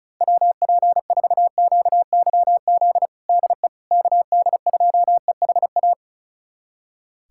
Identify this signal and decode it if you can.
Morse code, heard as a keyed tone, about 34 wpm, 700 Hz. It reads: WP4QYZ DE KD2EHA
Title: USA Callsign DE Callsign - 6 Characters - 35wpm